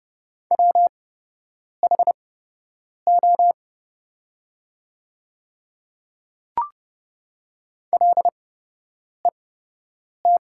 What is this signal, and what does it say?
Morse code, heard as a keyed tone, about 30 wpm, spaced as for 8 wpm, 700 Hz. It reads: WHO LET
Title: 8x Character Spacing - Top 300 Words - 30wpm